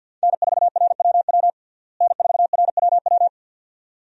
Morse code, heard as a keyed tone, about 49 wpm, 700 Hz. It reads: N4RWW N4RWW
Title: USA Call Signs - 5 Characters - Repeated 2x - 50wpm